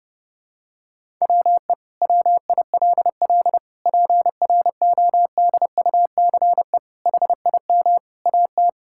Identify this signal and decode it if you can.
Morse code, heard as a keyed tone, about 30 wpm, 700 Hz. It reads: WE WILL PRODUCE HIM AT
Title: Sentences from Top 1000 Words - Review Part 5 - 30wpm